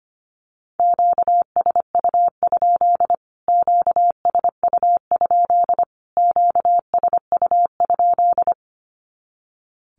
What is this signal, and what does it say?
Morse code, heard as a keyed tone, about 25 wpm, 700 Hz. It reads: QSU? QSU? QSU?
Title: Q-Codes Repeated 3x 25wpm